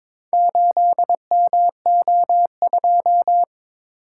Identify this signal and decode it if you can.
Morse code, heard as a keyed tone, about 22 wpm, 700 Hz. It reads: 8MO2
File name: practice / 22wpm / Four Letter-Number Combinations - 22wpm